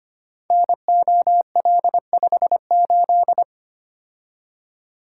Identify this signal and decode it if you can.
Morse code, heard as a keyed tone, about 25 wpm, 700 Hz. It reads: NOL58
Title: Code Groups - 5 Letter:Numbers - 25wpm